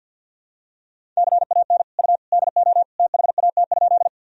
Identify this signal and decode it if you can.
Morse code, heard as a keyed tone, about 50 wpm, 700 Hz. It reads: CAN U DO THAT?
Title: Sentences from Top 100 Words and Abbreviations 50wpm